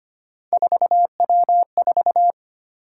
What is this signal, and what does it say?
Morse code, heard as a keyed tone, about 25 wpm, 700 Hz. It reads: 4W4
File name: Three Letter-Number Combinations - 25wpm